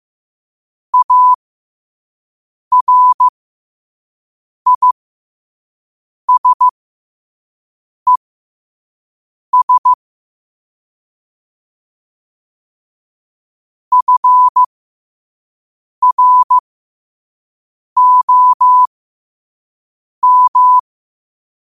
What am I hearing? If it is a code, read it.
Morse code, heard as a keyed tone, about 15 wpm, spaced as for 5 wpm, 1000 Hz. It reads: ARISES FROM
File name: qotdfiles / 5wpm